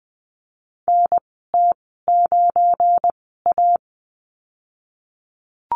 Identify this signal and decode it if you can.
Morse code, heard as a keyed tone, about 20 wpm, spaced as for 15 wpm, 700 Hz. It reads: NT9A
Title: 173 Lesson 28g - Number 9 - Call Signs 20wpm